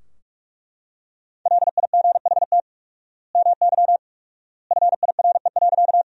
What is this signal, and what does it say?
Morse code, heard as a keyed tone, about 41 wpm, 700 Hz. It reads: LIGHT MY FIRE.
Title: 4x Word-Spacing - Sentences from Top 400 Words - 45wpm